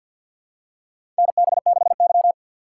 Morse code, heard as a keyed tone, about 47 wpm, 700 Hz. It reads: NB6Y
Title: USA Call Signs - 4 Characters - 6x Character Spacing - 50wpm